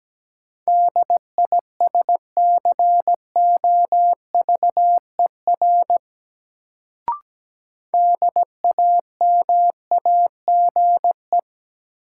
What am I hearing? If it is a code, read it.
Morse code, heard as a keyed tone, about 17 wpm, 700 Hz. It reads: DISCOVER DAMAGE
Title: Sentences from Top 1100 Words - New Word Drill - 17wpm